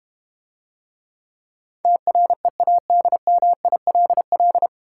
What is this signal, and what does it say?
Morse code, heard as a keyed tone, about 30 wpm, 700 Hz. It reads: TREADMILL
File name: practice / 25wpm / Difficult Words - Speed-Racing with Hints - 25wpm